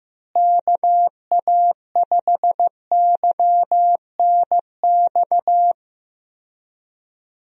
Morse code, heard as a keyed tone, about 15 wpm, 700 Hz. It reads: KA5YNX
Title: USA Call Signs - 6 Characters - Repeated 2x - 15wpm